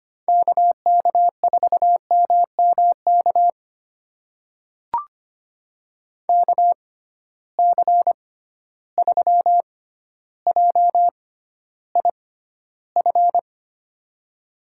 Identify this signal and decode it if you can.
Morse code, heard as a keyed tone, about 25 wpm, spaced as for 13 wpm, 700 Hz. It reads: KK4MMK KC3JIF
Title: USA Call Signs - 6 Characters - 6x Character Spacing - 25wpm